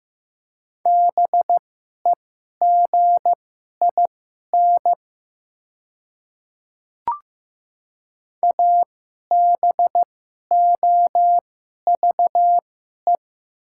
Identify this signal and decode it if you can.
Morse code, heard as a keyed tone, about 15 wpm, spaced as for 11 wpm, 690 Hz. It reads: BEGIN ABOVE
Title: 2x Character Spacing - Top 500 Words - 15wpm